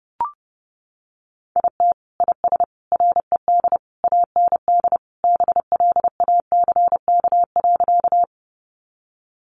Morse code, heard as a keyed tone, about 30 wpm, 700 Hz. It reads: IT IS RED AND BLACK.
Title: Additional Easy Phrases 30wpm